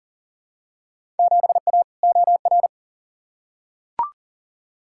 Morse code, heard as a keyed tone, about 40 wpm, 700 Hz. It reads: 7A OR
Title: Field Day Exchanges Encore - 40wpm